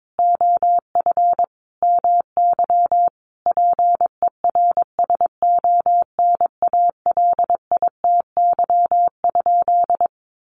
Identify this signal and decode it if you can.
Morse code, heard as a keyed tone, about 22 wpm, 700 Hz. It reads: OF MY PERSONALITY?
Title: Sentences from Top 1400 Words 22wpm